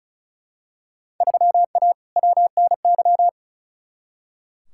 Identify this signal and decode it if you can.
Morse code, heard as a keyed tone, about 34 wpm, 700 Hz. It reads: 3A WNY
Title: Field Day Exchanges - 35wpm